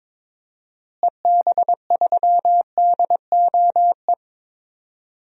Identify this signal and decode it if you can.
Morse code, heard as a keyed tone, about 22 wpm, 700 Hz. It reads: EB3DOE
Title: DXCC-100 Call Signs - 6 Characters - Repeated 3x - 22wpm